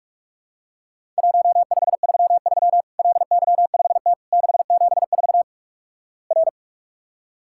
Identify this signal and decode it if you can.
Morse code, heard as a keyed tone, about 46 wpm, 700 Hz. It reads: 1533 LY5T 674 R
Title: Worked All Europe DX Contest - QTC Exchange - 45wpm